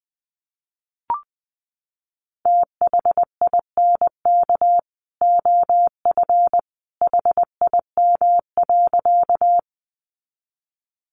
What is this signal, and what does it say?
Morse code, heard as a keyed tone, about 20 wpm, 700 Hz. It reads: THINK OF HIM.
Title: Sentences from Top 200 Words 20wpm